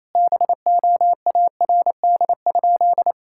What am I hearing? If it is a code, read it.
Morse code, heard as a keyed tone, about 28 wpm, 700 Hz. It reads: BOARD?
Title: Sentences from Top 1000 Words - Review Part 6 - 28wpm